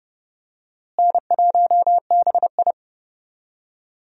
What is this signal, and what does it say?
Morse code, heard as a keyed tone, about 32 wpm, 700 Hz. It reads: N1BI
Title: USA Call Signs - 4 Characters - Repeated 2x - 30wpm